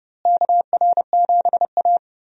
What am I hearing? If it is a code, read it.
Morse code, heard as a keyed tone, about 30 wpm, 700 Hz. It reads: KR7A
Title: USA Call Signs - 4 Characters - Repeated 2x - 30wpm